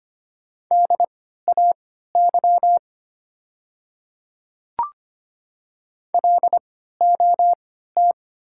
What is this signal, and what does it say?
Morse code, heard as a keyed tone, about 25 wpm, spaced as for 14 wpm, 700 Hz. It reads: DAY LOT
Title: 3x Character Spacing - Top 500 Words - 25wpm